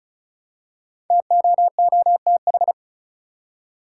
Morse code, heard as a keyed tone, about 36 wpm, 700 Hz. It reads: TOOTH
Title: Five Letter Words - 35wpm